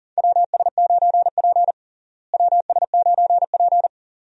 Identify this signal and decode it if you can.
Morse code, heard as a keyed tone, about 40 wpm, 700 Hz. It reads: WS9P WS9P
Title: USA Call Signs - 4 Characters - Repeated 2x - 40wpm